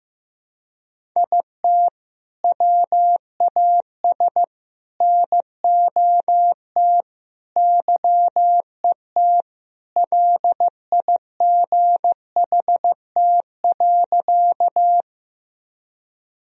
Morse code, heard as a keyed tone, about 15 wpm, 700 Hz. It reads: IT WAS NOT YET LIGHT.
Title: Sentences from Top 1000 Words - Review Part 1 - 15wpm